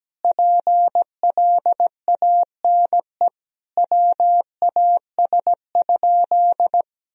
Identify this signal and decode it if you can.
Morse code, heard as a keyed tone, about 17 wpm, 700 Hz. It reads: PLANE WAS?
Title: Sentences from Top 1000 Words - Review Part 3 - 17wpm